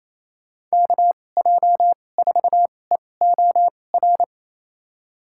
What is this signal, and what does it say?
Morse code, heard as a keyed tone, about 28 wpm, spaced as for 20 wpm, 700 Hz. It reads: KJ4EOR
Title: USA Call Signs - 6 Characters - 2x Character Spacing - 28wpm